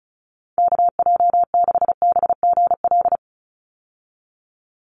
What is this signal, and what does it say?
Morse code, heard as a keyed tone, about 36 wpm, 700 Hz. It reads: KJ6BGL
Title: USA Call Signs 35wpm